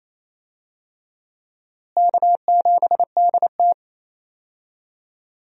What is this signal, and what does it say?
Morse code, heard as a keyed tone, about 28 wpm, 700 Hz. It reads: K7DT
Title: USA Call Signs - 4 Characters - Repeated 2x - 28wpm